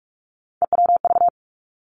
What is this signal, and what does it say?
Morse code, heard as a keyed tone, about 46 wpm, 700 Hz. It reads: EWV
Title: Three Challenging Letter-Number Combinations 45wpm